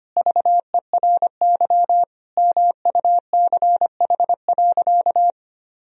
Morse code, heard as a keyed tone, about 25 wpm, 700 Hz. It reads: VERY MUCH.
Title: Sentences from Top 1000 Words - Review Part 3 - 25wpm